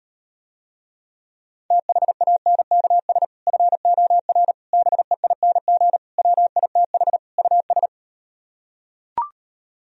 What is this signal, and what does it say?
Morse code, heard as a keyed tone, about 38 wpm, 700 Hz. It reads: THANKS FOR BEING WITH US